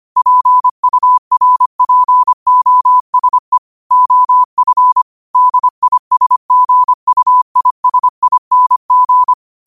Morse code, heard as a keyed tone, about 25 wpm, 1000 Hz. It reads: PURPOSE OF DISGUISING
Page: New quotes every day in morse code at 25 Words per minute.